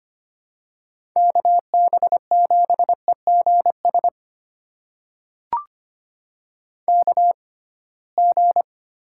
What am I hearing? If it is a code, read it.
Morse code, heard as a keyed tone, about 25 wpm, spaced as for 19 wpm, 700 Hz. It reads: KB7EGS KG
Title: USA Call Signs - 6x Character Spacing - 25wpm